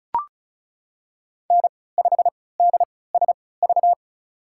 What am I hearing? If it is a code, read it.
Morse code, heard as a keyed tone, about 36 wpm, spaced as for 20 wpm, 700 Hz. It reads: N5DSV
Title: USA Call Signs - 5 Characters - 3x Character Spacing - 35wpm